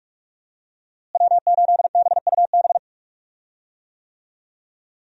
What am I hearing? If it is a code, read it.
Morse code, heard as a keyed tone, about 43 wpm, 700 Hz. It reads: W8BUB
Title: USA Call Signs - 5 Characters - Repeated 2x - 45wpm